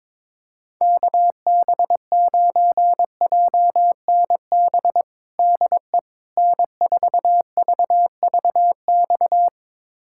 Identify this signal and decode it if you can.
Morse code, heard as a keyed tone, about 22 wpm, 700 Hz. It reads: KB9JNB DE N4VVX
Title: USA Callsign DE Callsign 22wpm